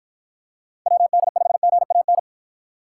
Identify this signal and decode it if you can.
Morse code, heard as a keyed tone, about 52 wpm, 700 Hz. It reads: WD5GAN